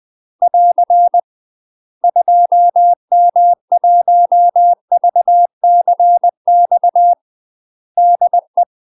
Morse code, heard as a keyed tone, about 20 wpm, 700 Hz. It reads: <AR> 2M1VCX DE
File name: UK MASTER QSOs TRAINING / 20WPM / UK MASTER QSOs 0055